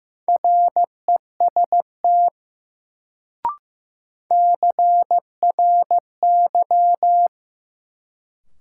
Morse code, heard as a keyed tone, about 15 wpm, 700 Hz. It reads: REST CRY
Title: Sentences from Top 500 Words - New Word Drill - 15wpm